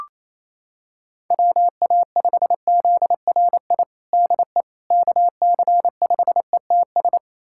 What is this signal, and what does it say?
Morse code, heard as a keyed tone, about 28 wpm, 700 Hz. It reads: WA5ZRI DE KC5ETS
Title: USA Callsign DE Callsign 28wpm